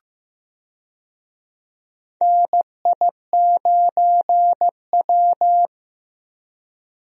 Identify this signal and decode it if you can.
Morse code, heard as a keyed tone, about 15 wpm, 700 Hz. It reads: NI9W
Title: USA Call Signs - 4 Characters - Repeated 3x - 15wpm